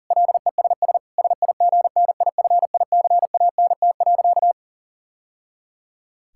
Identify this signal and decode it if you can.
Morse code, heard as a keyed tone, about 40 wpm, 700 Hz. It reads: LESS SIGNIFICANT.
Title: Sentences from Top 1000 Words - Review Part 6 - 40wpm